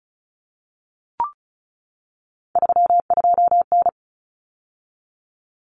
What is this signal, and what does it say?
Morse code, heard as a keyed tone, about 34 wpm, 700 Hz. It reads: 32N
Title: Common Cut Numbers - 3-Digit 35wpm